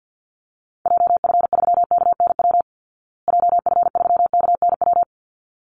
Morse code, heard as a keyed tone, about 49 wpm, 700 Hz. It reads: JF3KNW JF3KNW
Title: Call Signs Repeated 3x - 50wpm